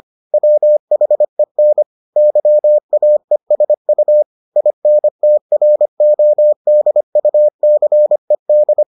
Morse code, heard as a keyed tone, about 25 wpm, 600 Hz. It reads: WHEN YAESU INTRODUCED